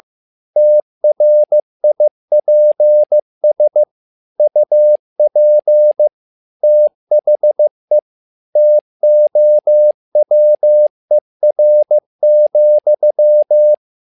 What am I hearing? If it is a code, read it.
Morse code, heard as a keyed tone, about 15 wpm, 600 Hz. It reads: TRIPS UP THE TOWER,
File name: learning center / tower trouble / tower trouble 15wpm